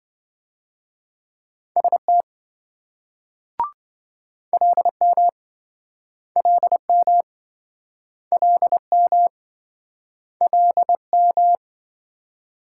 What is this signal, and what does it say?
Morse code, heard as a keyed tone, about 26 wpm, 700 Hz. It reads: ST LM LM LM LM